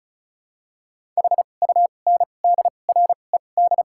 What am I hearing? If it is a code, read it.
Morse code, heard as a keyed tone, about 36 wpm, spaced as for 25 wpm, 700 Hz. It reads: HUNDRED
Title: 2x Character Spacing - Top 500 Words - 35wpm